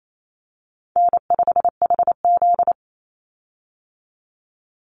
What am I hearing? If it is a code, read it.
Morse code, heard as a keyed tone, about 28 wpm, 700 Hz. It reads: N5HZ
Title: Call Signs - 28wpm